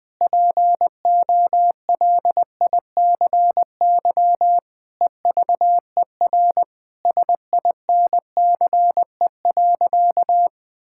Morse code, heard as a keyed tone, about 20 wpm, 700 Hz. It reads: POLICY EVER SINCE.
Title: Sentences from Top 1000 Words - Review Part 5 - 20wpm